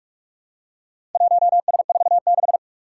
Morse code, heard as a keyed tone, about 46 wpm, 700 Hz. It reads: 1S46
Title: Four Challenging Letter-Number Combinations 45wpm